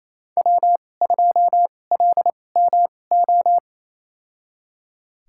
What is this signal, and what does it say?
Morse code, heard as a keyed tone, about 28 wpm, spaced as for 20 wpm, 700 Hz. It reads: W2LMO
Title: USA Call Signs - 5 Characters - 2x Character Spacing - 28wpm